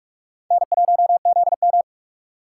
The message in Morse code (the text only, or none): N1ZM